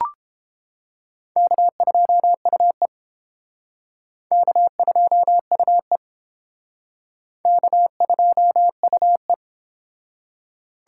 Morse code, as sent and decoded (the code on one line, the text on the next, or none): -.- ..--- ..- . / -.- ..--- ..- . / -.- ..--- ..- .
K2UE K2UE K2UE